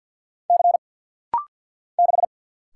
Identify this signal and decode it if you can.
Morse code, heard as a keyed tone, about 50 wpm, 700 Hz. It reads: C 6
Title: Single Letter-Number Alt - Rapid-Fire 50wpm